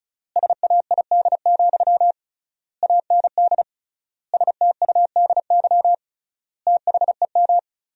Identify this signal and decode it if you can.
Morse code, heard as a keyed tone, about 36 wpm, 700 Hz. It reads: SAID, AND STUDY THEM
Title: 3x Word-Spacing - Sentences from Top 300 Words - 35wpm